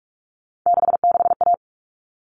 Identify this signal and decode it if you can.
Morse code, heard as a keyed tone, about 44 wpm, 700 Hz. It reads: 66A
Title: All Cut Numbers - 3-Digit 45wpm